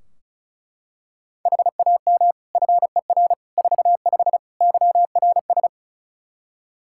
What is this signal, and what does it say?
Morse code, heard as a keyed tone, about 36 wpm, 700 Hz. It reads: HAM FER 45 YRS
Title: Practice QSO - Exchange 3 - More Info - 35wpm